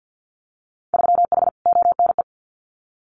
Morse code, heard as a keyed tone, about 50 wpm, 700 Hz. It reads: 3H ONE
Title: Winter Field Day Exchanges Encore 50wpm